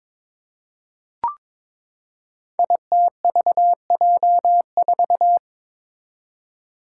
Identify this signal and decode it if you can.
Morse code, heard as a keyed tone, about 22 wpm, 700 Hz. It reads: ITVJ4
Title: Five Challenging Letter-Number Combinations 22wpm